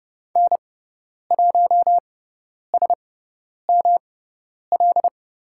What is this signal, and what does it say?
Morse code, heard as a keyed tone, about 30 wpm, spaced as for 10 wpm, 700 Hz. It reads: N1SML
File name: Lesson 18g - Letter m - Call Signs 30wpm